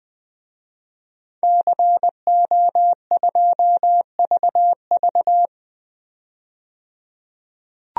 Morse code, heard as a keyed tone, about 20 wpm, 700 Hz. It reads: CO2VV
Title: DXCC-100 Call Signs - 5 Characters - 20wpm